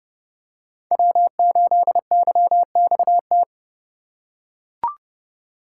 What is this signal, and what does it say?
Morse code, heard as a keyed tone, about 28 wpm, 700 Hz. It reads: W8YXT